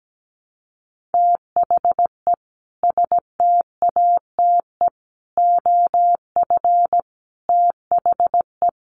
Morse code, heard as a keyed tone, about 17 wpm, 700 Hz. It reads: THE STATE OF THE
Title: Sets of 5 Words Encore 17wpm